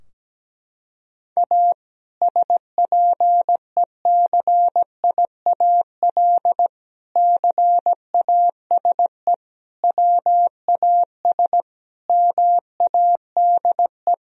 Morse code, heard as a keyed tone, about 17 wpm, 700 Hz. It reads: A SPECIAL CASE WAS MADE
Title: Sentences from Top 1000 Words - Review Part 3 - 17wpm